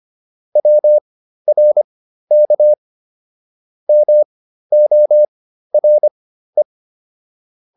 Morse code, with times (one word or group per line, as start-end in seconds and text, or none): 0.6-2.7 WRK
3.9-6.6 MORE